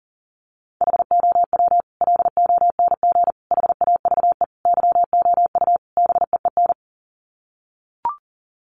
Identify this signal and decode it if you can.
Morse code, heard as a keyed tone, about 40 wpm, 700 Hz. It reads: HOW LONG HAVE YOU BEEN
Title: Sets of 5 Words Encore 40wpm